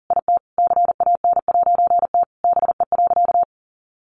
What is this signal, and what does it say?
Morse code, heard as a keyed tone, about 40 wpm, 700 Hz. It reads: IT CAN'T BE.
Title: A Sound Of Thunder 40wpm